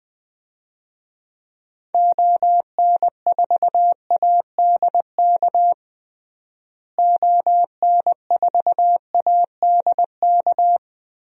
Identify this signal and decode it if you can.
Morse code, heard as a keyed tone, about 21 wpm, 700 Hz. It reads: ON4ADK ON4ADK
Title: DXCC-100 Call Signs - 6 Characters - Repeated 3x - 20wpm